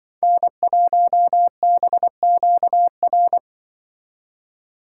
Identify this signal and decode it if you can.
Morse code, heard as a keyed tone, about 25 wpm, 700 Hz. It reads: N1BQR